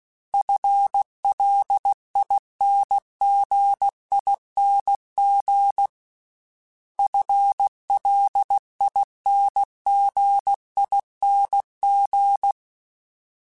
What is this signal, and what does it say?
Morse code, heard as a keyed tone, about 16 wpm, 800 Hz. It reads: FLINGING FLINGING
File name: FL / FL mp3s / FL Prefixes 3x2 Twice @15